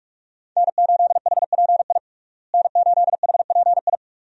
Morse code, heard as a keyed tone, about 46 wpm, 700 Hz. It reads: N8HPI N8HPI
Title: USA Call Signs - 5 Characters - Repeated 2x - 45wpm